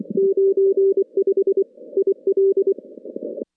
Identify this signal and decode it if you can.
Morse code, heard as a keyed tone, about 24 wpm, 390 Hz. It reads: 95 IL
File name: contests / CW-SS-k5tr- -2200